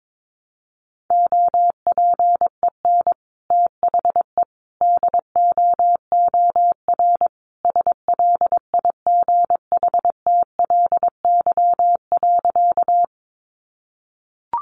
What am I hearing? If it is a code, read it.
Morse code, heard as a keyed tone, about 22 wpm, 700 Hz. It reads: OPEN THE DOOR SLIGHTLY.
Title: Sentences from Top 1300 Words 22wpm